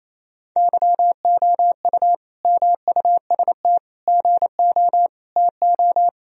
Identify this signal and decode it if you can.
Morse code, heard as a keyed tone, about 28 wpm, 700 Hz. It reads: YOU MUST GO TO
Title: Sentences from Top 1000 Words - Review Part 1 - 28wpm